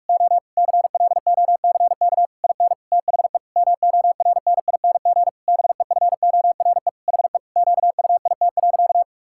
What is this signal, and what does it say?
Morse code, heard as a keyed tone, about 41 wpm, 700 Hz. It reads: O CLOCK IN THE MORNING BEFORE HE QUIT.
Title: Pangram Challenge v2 - 45wpm